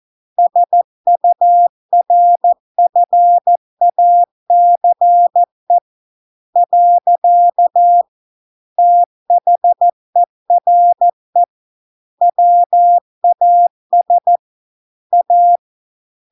Morse code, heard as a keyed tone, about 14 wpm, 700 Hz. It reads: SURFACE . THERE WAS A